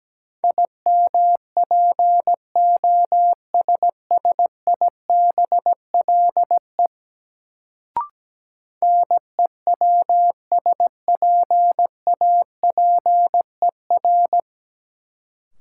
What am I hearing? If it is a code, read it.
Morse code, heard as a keyed tone, about 17 wpm, 700 Hz. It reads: IMPOSSIBLE NEWSPAPER
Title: Sentences from Top 1100 Words - New Word Drill Alt - 17wpm